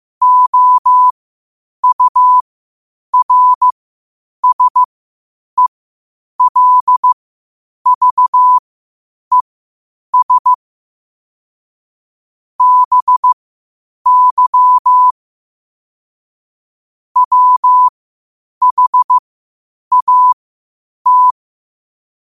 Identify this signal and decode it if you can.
Morse code, heard as a keyed tone, about 15 wpm, spaced as for 8 wpm, 1000 Hz. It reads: OURSELVES BY WHAT
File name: qotdfiles / 8wpm